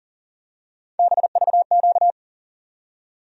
Morse code, heard as a keyed tone, about 40 wpm, 700 Hz. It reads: BVQ
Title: USA Call Sign Postfixes - 3 Letters 40wpm